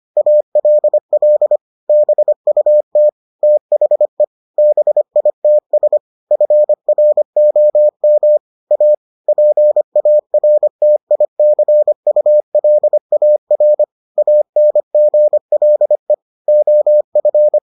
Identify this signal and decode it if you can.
Morse code, heard as a keyed tone, about 25 wpm, 600 Hz. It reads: ALL BUT THE BITS FROM A PARTICULAR ANGLE OF